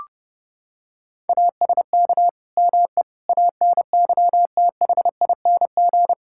Morse code, heard as a keyed tone, about 32 wpm, 700 Hz. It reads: ASK ME ANYTHING
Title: Sentences from Top 1600 Words 30wpm